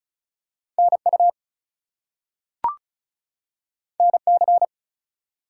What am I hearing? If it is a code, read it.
Morse code, heard as a keyed tone, about 36 wpm, 700 Hz. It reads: NU NC
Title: US States and Canadian Provinces 35wpm